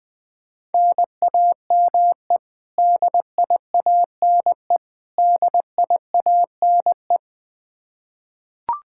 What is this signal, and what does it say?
Morse code, heard as a keyed tone, about 21 wpm, 700 Hz. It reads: NAME DIANE DIANE
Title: Practice QSO - Exchange 1 - Beginning - 20wpm